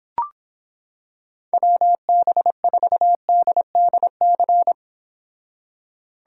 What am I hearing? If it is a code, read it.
Morse code, heard as a keyed tone, about 26 wpm, 700 Hz. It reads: WB4DDC